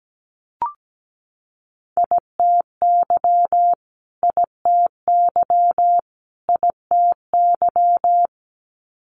Morse code, heard as a keyed tone, about 17 wpm, 700 Hz. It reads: ITY ITY ITY
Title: Multi-letter Phonemes Repeated 3x 17wpm